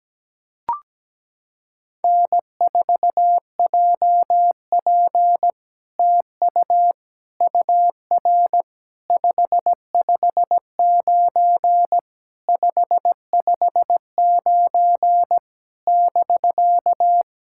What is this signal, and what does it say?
Morse code, heard as a keyed tone, about 17 wpm, 700 Hz. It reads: N4JP TU UR 559 559 <BK>
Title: SOTA QSO Practice 17wpm